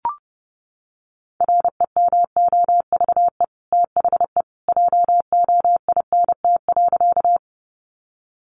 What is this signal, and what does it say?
Morse code, heard as a keyed tone, about 30 wpm, 700 Hz. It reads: REMOVE THE JOINT.
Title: Sentences from Top 1300 Words 30wpm